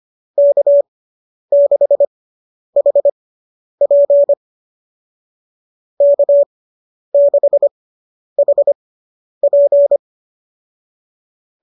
Morse code, heard as a keyed tone, about 25 wpm, spaced as for 10 wpm, 590 Hz. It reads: K6HP K6HP